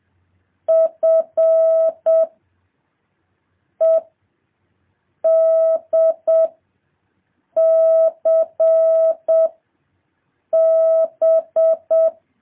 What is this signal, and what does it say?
Morse code, heard as a keyed tone, about 7 wpm, 650 Hz. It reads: F E D C B
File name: jh8cbh / 2023cw